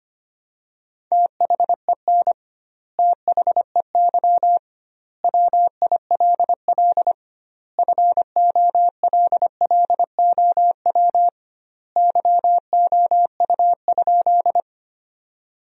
Morse code, heard as a keyed tone, about 25 wpm, 700 Hz. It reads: THEN THEY WILL FOLLOW YOU?
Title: 2x Word-Spacing - Sentences from Top 200 Words - 25wpm